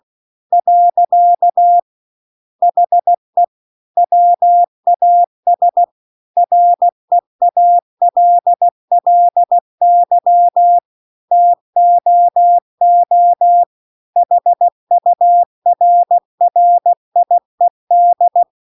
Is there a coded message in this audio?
Yes: Morse code from 0.5 to 18.4 s, heard as a keyed tone, about 16 wpm, 700 Hz. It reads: . HE WAS REALLY TOO HURRIED